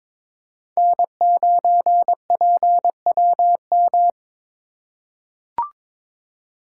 Morse code, heard as a keyed tone, about 22 wpm, 700 Hz. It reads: N9PWM
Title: USA Call Signs - 5 Characters - 4x Character Spacing - 22wpm